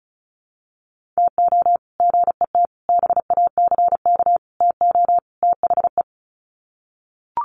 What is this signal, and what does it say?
Morse code, heard as a keyed tone, about 34 wpm, 700 Hz. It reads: TO GET BACK TO THE
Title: Sets of 5 Words Encore 35wpm